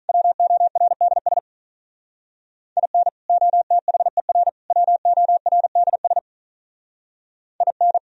Morse code, heard as a keyed tone, about 44 wpm, 690 Hz. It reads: WORDS IN OTHER WORDS IN